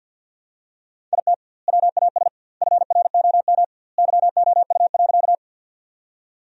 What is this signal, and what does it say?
Morse code, heard as a keyed tone, about 50 wpm, 700 Hz. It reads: IT WAS FROM YOU.
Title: 2x Word-Spacing - Sentences from Top 100 Words - 50wpm